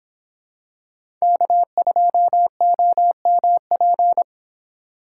Morse code, heard as a keyed tone, about 26 wpm, 700 Hz. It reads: K2OMP
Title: Speed Racing - USA 1x3 Call Signs 22wpm